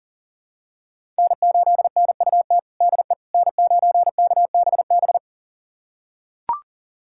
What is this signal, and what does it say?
Morse code, heard as a keyed tone, about 37 wpm, 700 Hz. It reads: N8NUT DE N9KBB
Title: USA Callsign DE Callsign - 5 Characters - 40wpm